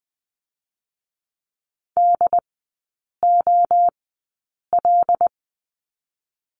Morse code, heard as a keyed tone, about 19 wpm, spaced as for 8 wpm, 690 Hz. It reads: DOL
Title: 087 Lesson 15e - Number 5 - Three Characters 20wpm